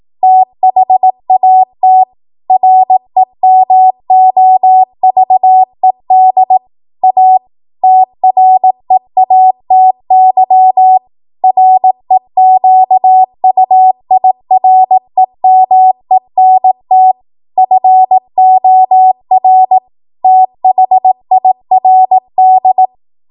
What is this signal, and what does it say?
Morse code, heard as a keyed tone, about 18 wpm, 750 Hz. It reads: THAT REMOVED A TREATY REQUIREMENT FOR THIRD
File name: Morse / 18WPM